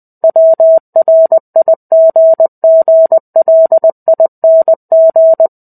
Morse code, heard as a keyed tone, about 20 wpm, 650 Hz. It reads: WRIGGLING
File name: 05-Chapter-05 20 WPM